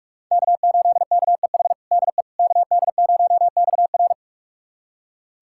Morse code, heard as a keyed tone, about 41 wpm, 700 Hz. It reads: K8KEH DE KD0XR
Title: USA Callsign DE Callsign - 5 Characters - 45wpm